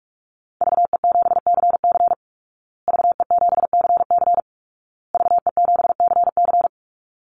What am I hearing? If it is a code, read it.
Morse code, heard as a keyed tone, about 44 wpm, 700 Hz. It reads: VE7CC VE7CC VE7CC
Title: Call Signs Repeated 2x - 45wpm